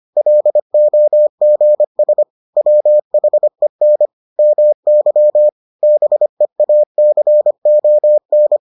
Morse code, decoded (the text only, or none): LOGS WHEN MY BEACON